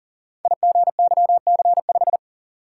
Morse code, heard as a keyed tone, about 38 wpm, 690 Hz. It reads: IGYC5
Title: Code Groups - 5 Letter:Numbers - 40wpm